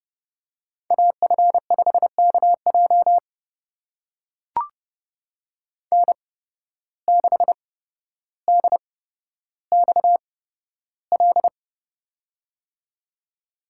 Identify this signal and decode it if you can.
Morse code, heard as a keyed tone, about 30 wpm, spaced as for 13 wpm, 700 Hz. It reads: AF5KJ N6DXL
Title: USA Call Signs - 5 Characters - 8x Character Spacing - 30wpm